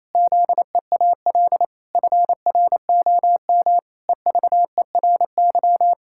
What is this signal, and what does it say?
Morse code, heard as a keyed tone, about 28 wpm, 700 Hz. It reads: ZEAL FROM EVERY
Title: Pangram Challenge v2 - 28wpm